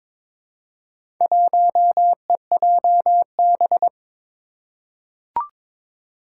Morse code, heard as a keyed tone, about 22 wpm, 700 Hz. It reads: 1EJB
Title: Four Challenging Letter-Number Combinations 22wpm